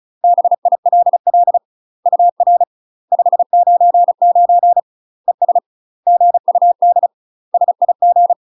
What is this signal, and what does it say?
Morse code, heard as a keyed tone, about 36 wpm, 700 Hz. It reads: BILL UR 599 ES GUD SIG
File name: UK MASTER QSOs TRAINING / 35WPM / UK MASTER QSOs 0016